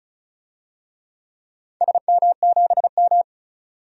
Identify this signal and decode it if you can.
Morse code, heard as a keyed tone, about 36 wpm, 700 Hz. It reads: SM7M
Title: DXCC-100 Call Signs - 4 Characters - 35wpm